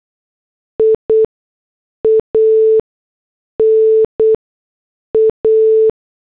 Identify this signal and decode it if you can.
Morse code, heard as a keyed tone, about 8 wpm, 440 Hz. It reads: I A N A
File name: morse / headlines